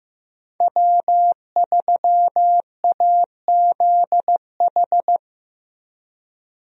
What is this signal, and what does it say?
Morse code, heard as a keyed tone, about 15 wpm, 700 Hz. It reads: W3AZH
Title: USA Call Signs - 5 Characters - 15wpm